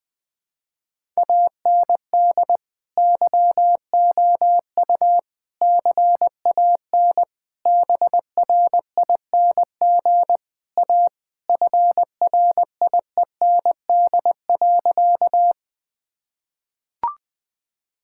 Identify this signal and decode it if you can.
Morse code, heard as a keyed tone, about 21 wpm, 700 Hz. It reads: AND YOU CAN BRING A FRIEND.
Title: Sentences with 2-Word Phrases - Spoken Beginning - 20wpm